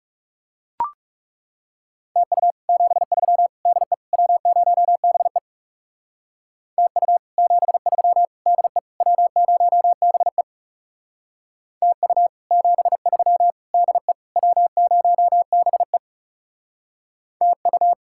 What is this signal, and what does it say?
Morse code, heard as a keyed tone, about 37 wpm, 700 Hz. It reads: TU 73 DE W0BE TU 73 DE W0BE TU 73 DE W0BE TU
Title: Speed-Racing Common QSO Phrases - 30wpm